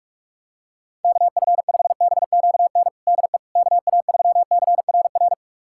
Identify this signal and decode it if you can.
Morse code, heard as a keyed tone, about 42 wpm, 700 Hz. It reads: KF5BQN DE KA3CRR
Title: USA Callsign DE Callsign - 6 Characters - 45wpm